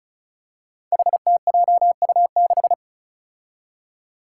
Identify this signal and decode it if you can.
Morse code, heard as a keyed tone, about 36 wpm, 700 Hz. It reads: HTJU6
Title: Five Challenging Letter-Number Combinations 35wpm